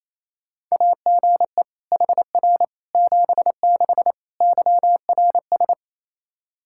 Morse code, heard as a keyed tone, about 28 wpm, 700 Hz. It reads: AGE HR 76 YRS
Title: Practice QSO - Exchange 3 - More Info - 28wpm